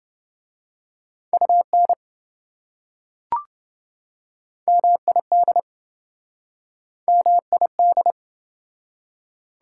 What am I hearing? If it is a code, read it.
Morse code, heard as a keyed tone, about 26 wpm, 700 Hz. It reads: UN MID MID